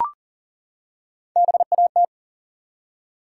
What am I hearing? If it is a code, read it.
Morse code, heard as a keyed tone, about 40 wpm, 700 Hz. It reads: BAT
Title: Three Letter Words - 40wpm